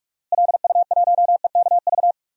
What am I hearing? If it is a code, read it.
Morse code, heard as a keyed tone, about 45 wpm, 700 Hz. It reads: LU1EKV
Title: DXCC-100 Call Signs - 6 Characters - Repeated 3x - 45wpm